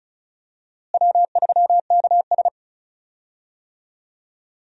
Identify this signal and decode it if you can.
Morse code, heard as a keyed tone, about 36 wpm, 700 Hz. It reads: W3KS
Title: USA Call Signs - 4 Characters - Repeated 3x - 35wpm